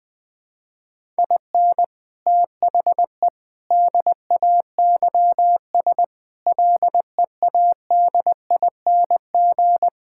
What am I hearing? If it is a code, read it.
Morse code, heard as a keyed tone, about 21 wpm, 700 Hz. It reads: IN THE DAYS LEADING
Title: Sets of 6 Words - 20wpm